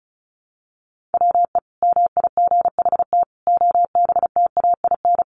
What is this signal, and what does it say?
Morse code, heard as a keyed tone, about 34 wpm, 700 Hz. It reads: WE MIGHT OBTAIN
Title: Sentences from Top 1300 Words 35wpm